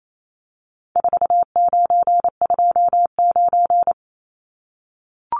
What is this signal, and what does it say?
Morse code, heard as a keyed tone, about 28 wpm, 700 Hz. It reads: 4929
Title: Numbers - 4-Digit 28wpm